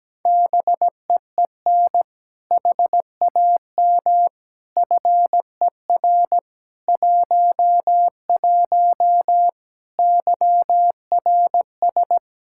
Morse code, heard as a keyed tone, about 17 wpm, 700 Hz. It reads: BEEN HAM FER 11 YRS
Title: Practice QSO - Exchange 3 - More Info - 17wpm